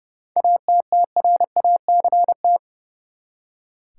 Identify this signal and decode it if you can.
Morse code, heard as a keyed tone, about 30 wpm, 700 Hz. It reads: ATTRACT
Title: Seven Letter Words - 30wpm